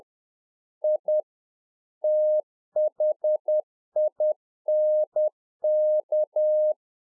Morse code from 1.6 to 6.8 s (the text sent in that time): THINK